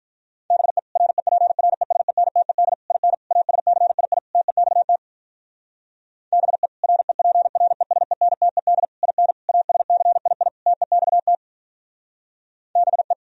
Morse code, heard as a keyed tone, about 44 wpm, 700 Hz. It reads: BE REPRESENTED IN ASCII TEXT BE REPRESENTED IN ASCII TEXT BE